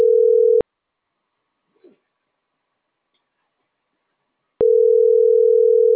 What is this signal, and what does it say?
Ringback tone (North America), 440 + 480 Hz.